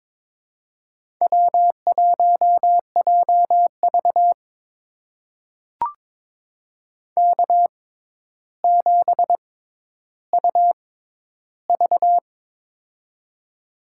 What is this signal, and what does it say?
Morse code, heard as a keyed tone, about 22 wpm, spaced as for 11 wpm, 700 Hz. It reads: W1JV K7UV
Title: USA Call Signs - 4 Characters - 6x Character Spacing - 22wpm